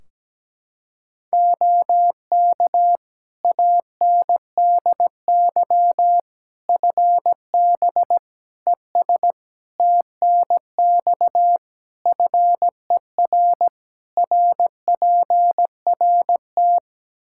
Morse code, heard as a keyed tone, about 17 wpm, 700 Hz. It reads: OK ANDY FB ES TNX FER RPRT
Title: Practice QSO - Exchange 2 - Info - 17wpm